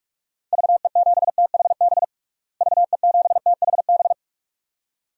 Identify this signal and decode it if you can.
Morse code, heard as a keyed tone, about 46 wpm, 700 Hz. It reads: VE7THB VE7THB
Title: DXCC-100 Call Signs - 6 Characters - Repeated 2x - 45wpm